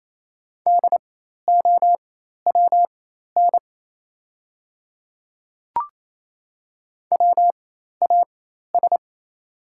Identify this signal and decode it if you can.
Morse code, heard as a keyed tone, about 28 wpm, spaced as for 13 wpm, 700 Hz. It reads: DOWN WAS
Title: 4x Character Spacing - Top 100 Words - 28wpm